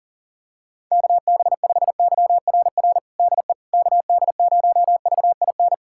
Morse code, heard as a keyed tone, about 37 wpm, 700 Hz. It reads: KB5YRR DE KD0VIN
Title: USA Callsign DE Callsign 40wpm